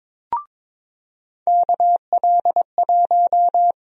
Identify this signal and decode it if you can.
Morse code, heard as a keyed tone, about 22 wpm, 700 Hz. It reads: KL1
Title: USA Call Sign Prefixes - 22wpm